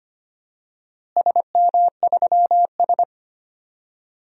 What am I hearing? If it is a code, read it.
Morse code, heard as a keyed tone, about 25 wpm, 700 Hz. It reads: SM3S
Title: DXCC-100 Call Signs - 4 Characters - Repeated 3x - 25wpm